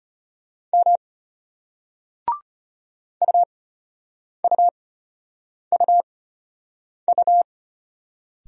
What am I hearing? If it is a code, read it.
Morse code, heard as a keyed tone, about 34 wpm, spaced as for 8 wpm, 700 Hz. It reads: M UUUU